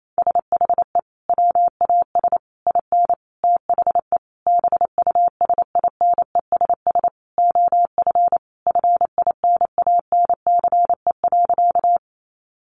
Morse code, heard as a keyed tone, about 28 wpm, 700 Hz. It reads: SHE WAS IN THE BUSINESS OF FINANCE.
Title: Sentences from Top 1500 Words 28wpm